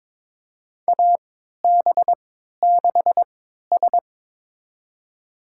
Morse code, heard as a keyed tone, about 22 wpm, spaced as for 13 wpm, 700 Hz. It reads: AB6S
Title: USA Call Signs - 4 Characters - 3x Character Spacing - 22wpm